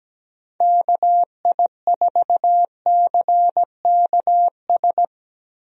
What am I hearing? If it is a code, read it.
Morse code, heard as a keyed tone, about 17 wpm, 700 Hz. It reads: KI4CKS
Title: USA Call Signs - 6 Characters - Repeated 3x - 17wpm